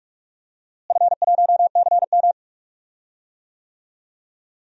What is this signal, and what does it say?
Morse code, heard as a keyed tone, about 41 wpm, 700 Hz. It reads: F1CM
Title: DXCC-100 Call Signs - 4 Characters - Repeated 2x - 45wpm